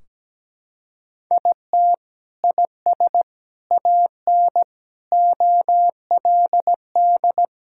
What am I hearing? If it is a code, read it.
Morse code, heard as a keyed tone, about 17 wpm, 700 Hz. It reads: IT IS AN OLD
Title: Sentences from Top 1600 Words 17wpm